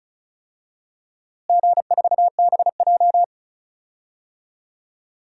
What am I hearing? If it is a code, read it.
Morse code, heard as a keyed tone, about 36 wpm, 700 Hz. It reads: G4BJ
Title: DXCC-100 Call Signs - 4 Characters - Repeated 2x - 35wpm